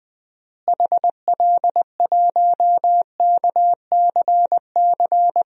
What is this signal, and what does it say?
Morse code, heard as a keyed tone, about 20 wpm, 700 Hz. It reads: HL1KCC
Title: DXCC-100 Call Signs - 6 Characters - Repeated 3x - 20wpm